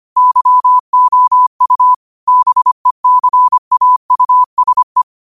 Morse code, heard as a keyed tone, about 25 wpm, 1000 Hz. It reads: YOU BECAUSE
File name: qotdfiles / 25wpm